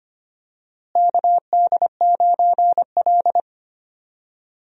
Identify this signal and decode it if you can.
Morse code, heard as a keyed tone, about 25 wpm, 700 Hz. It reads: KD9L
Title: USA Call Signs - 4 Characters - Repeated 3x - 25wpm